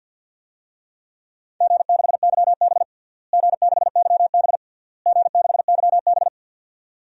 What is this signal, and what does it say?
Morse code, heard as a keyed tone, about 50 wpm, 700 Hz. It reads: G6YB G6YB G6YB
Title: DXCC-100 Call Signs - 4 Characters - Repeated 3x - 50wpm